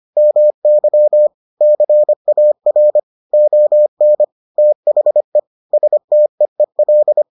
Morse code, heard as a keyed tone, about 25 wpm, 600 Hz. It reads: MY CAR ON THE STEEL